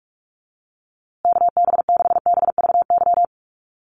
Morse code, heard as a keyed tone, about 44 wpm, 700 Hz. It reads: KB6BVY
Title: USA Call Signs Repeated 2x 45wpm